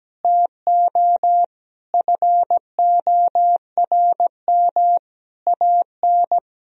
Morse code, heard as a keyed tone, about 17 wpm, 700 Hz. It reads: TO FORM AN
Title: Sentences from Top 1000 Words - Review Part 1 - 17wpm